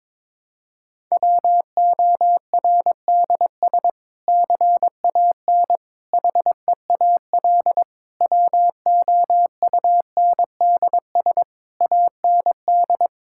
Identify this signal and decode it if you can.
Morse code, heard as a keyed tone, about 22 wpm, 700 Hz. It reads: WORDS CAN HEAL WOUNDS AND
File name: Difficult Sentences with Hints - 22wpm